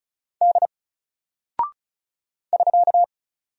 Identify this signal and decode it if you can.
Morse code, heard as a keyed tone, about 36 wpm, 700 Hz. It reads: D <SK>
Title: Single Character (Letter:Number:Punctuation:Prosign) 35wpm